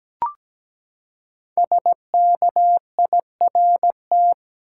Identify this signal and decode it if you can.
Morse code, heard as a keyed tone, about 17 wpm, 700 Hz. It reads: SKIRT
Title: Five Letter Words - 17wpm